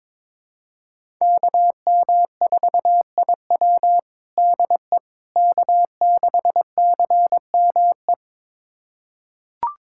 Morse code, heard as a keyed tone, about 22 wpm, 700 Hz. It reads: KM4IW DE K6CME
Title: USA Callsign DE Callsign - 5 Characters - 22wpm